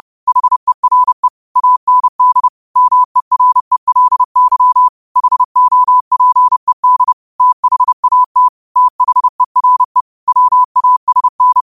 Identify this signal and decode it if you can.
Morse code, heard as a keyed tone, about 30 wpm, 1000 Hz. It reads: HERE AND MERELY HOPED THAT THERE WASN
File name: qotdfiles / 30wpm